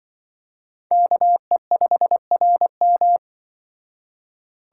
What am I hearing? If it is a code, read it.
Morse code, heard as a keyed tone, about 24 wpm, 700 Hz. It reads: KE5RM
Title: Speed Racing - USA 2x2 Call Signs 20wpm